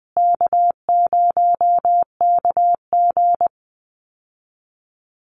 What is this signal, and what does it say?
Morse code, heard as a keyed tone, about 20 wpm, 700 Hz. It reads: K0KG
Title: Easy Call Signs Repeated 2x - 20wpm